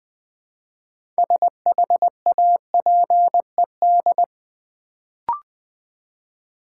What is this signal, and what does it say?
Morse code, heard as a keyed tone, about 20 wpm, 700 Hz. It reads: SHAPED
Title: Build A Word - 2 Letters At A Time - 6-Letter Words - 20wpm